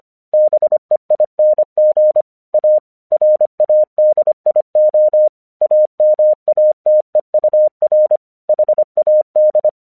Morse code, heard as a keyed tone, about 25 wpm, 610 Hz. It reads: BEING A RADIO AMATEUR HAD
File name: 1 English 25wpm / AQC05